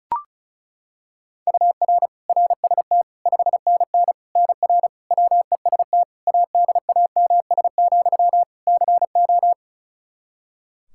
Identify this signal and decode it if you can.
Morse code, heard as a keyed tone, about 34 wpm, 700 Hz. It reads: UR RST 5NN NR WEST ADAMS, CO
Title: Practice QSO - Exchange 1 - Beginning - 35wpm